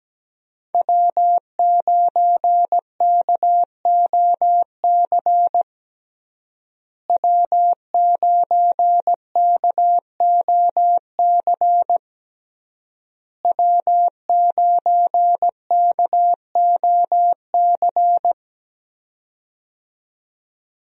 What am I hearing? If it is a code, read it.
Morse code, heard as a keyed tone, about 17 wpm, 700 Hz. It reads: W9KOC W9KOC W9KOC
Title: USA Call Signs - 5 Characters - Repeated 3x - 17wpm